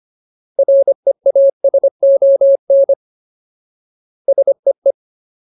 Morse code, heard as a keyed tone, about 25 wpm, 550 Hz. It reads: REASON SEE
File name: Ultimate Word List / 25WPM Ultimate